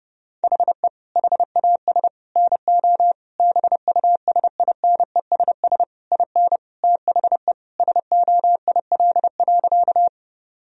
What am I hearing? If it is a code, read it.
Morse code, heard as a keyed tone, about 30 wpm, 700 Hz. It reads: HE HAS NO BUSINESS IN THE SOIL.
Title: Sentences from Top 1000 Words - Review Part 4 - 30wpm